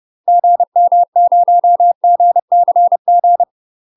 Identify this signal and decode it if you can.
Morse code, heard as a keyed tone, about 30 wpm, 700 Hz. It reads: GM0GCG
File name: UK MASTER QSOs TRAINING / 30WPM / UK MASTER QSOs 0038